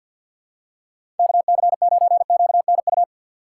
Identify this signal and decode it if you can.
Morse code, heard as a keyed tone, about 52 wpm, 700 Hz. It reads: KC9QNU
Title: USA Call Signs - 6 Characters - 3x Character Spacing - 50wpm